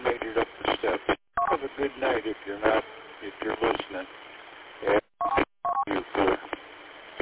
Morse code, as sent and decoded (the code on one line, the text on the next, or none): none